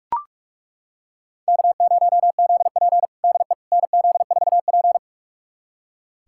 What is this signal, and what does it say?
Morse code, heard as a keyed tone, about 42 wpm, 700 Hz. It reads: K0ZP DE NZ4P
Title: USA Callsign DE Callsign - 4 Characters - 45wpm